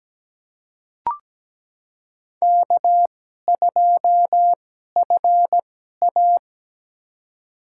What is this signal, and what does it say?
Morse code, heard as a keyed tone, about 17 wpm, spaced as for 12 wpm, 700 Hz. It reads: K2FA
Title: USA Call Signs - 4 Characters - 2x Character Spacing - 17wpm